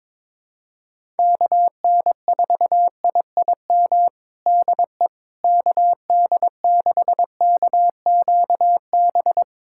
Morse code, heard as a keyed tone, about 22 wpm, 700 Hz. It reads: KN4IIM DE KD6KQB
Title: USA Callsign DE Callsign 22wpm